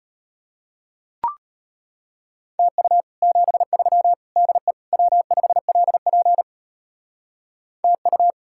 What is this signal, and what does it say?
Morse code, heard as a keyed tone, about 32 wpm, 700 Hz. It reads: TU 73 DE W5LP TU